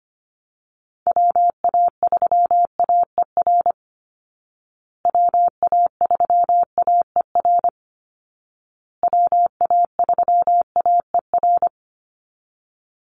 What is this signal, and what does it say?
Morse code, heard as a keyed tone, about 25 wpm, 700 Hz. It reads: WA3AER WA3AER WA3AER
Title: Call Signs Repeated 2x - 25wpm